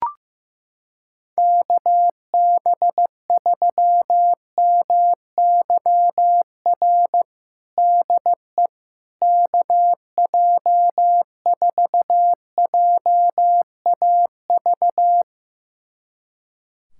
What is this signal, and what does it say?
Morse code, heard as a keyed tone, about 16 wpm, 700 Hz. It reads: KB3MYR DE KJ4JAV
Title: USA Callsign DE Callsign - 6 Characters - 15wpm